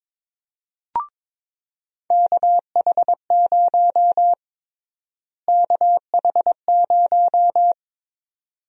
Morse code, heard as a keyed tone, about 22 wpm, 700 Hz. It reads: KH0 KH0
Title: USA Call Sign Prefixes - Repeated 2x - 22wpm